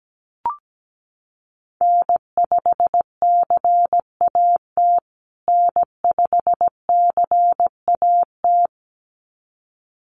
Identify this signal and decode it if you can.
Morse code, heard as a keyed tone, about 17 wpm, 700 Hz. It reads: N5CAT N5CAT
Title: USA Call Signs Repeated 2x 17wpm